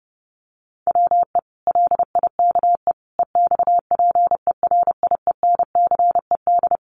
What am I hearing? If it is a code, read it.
Morse code, heard as a keyed tone, about 30 wpm, 700 Hz. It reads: WE LIKE EXPERIENCED
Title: Sentences from Top 1100 Words 30wpm